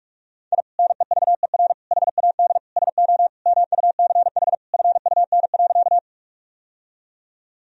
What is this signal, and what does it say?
Morse code, heard as a keyed tone, about 46 wpm, 700 Hz. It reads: I NEVER HAD SO MUCH FUN.
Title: Sentences from Top 1000 Words - Review Part 4 - 45wpm